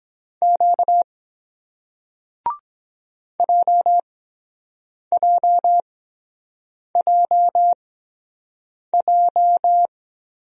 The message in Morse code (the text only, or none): Q JJJJ